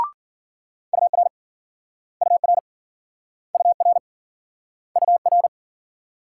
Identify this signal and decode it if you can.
Morse code, heard as a keyed tone, about 51 wpm, 700 Hz. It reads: UR UR UR UR